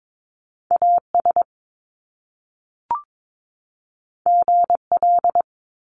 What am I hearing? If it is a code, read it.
Morse code, heard as a keyed tone, about 22 wpm, 700 Hz. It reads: AS GL
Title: Top Two Letter Combinations in Words 22wpm